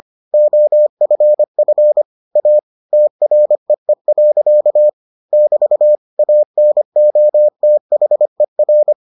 Morse code, heard as a keyed tone, about 25 wpm, 600 Hz. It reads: OFF A TREE. <BT> ANOTHER